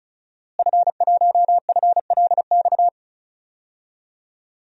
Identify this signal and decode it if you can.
Morse code, heard as a keyed tone, about 36 wpm, 700 Hz. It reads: F1FLX
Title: DXCC-100 Call Signs - 5 Characters - Repeated 3x - 35wpm